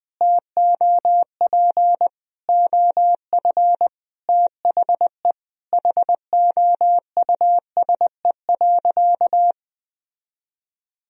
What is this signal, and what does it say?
Morse code, heard as a keyed tone, about 19 wpm, 700 Hz. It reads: TOP OF THE HOUSE.
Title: Sentences from Top 1000 Words - Review Part 2 - 20wpm